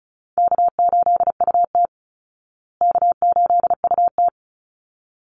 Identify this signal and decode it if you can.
Morse code, heard as a keyed tone, about 34 wpm, 700 Hz. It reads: K8UT K8UT
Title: Easy Call Signs Repeated 2x - 35wpm